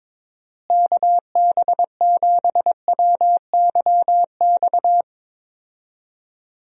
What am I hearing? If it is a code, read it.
Morse code, heard as a keyed tone, about 22 wpm, 700 Hz. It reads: KB7WYX
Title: USA Call Signs - 6 Characters - Repeated 2x - 22wpm